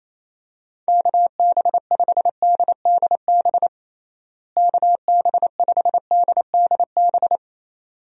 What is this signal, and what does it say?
Morse code, heard as a keyed tone, about 28 wpm, 700 Hz. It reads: KB5DDB KB5DDB
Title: USA Call Signs - 6 Characters - Repeated 2x - 28wpm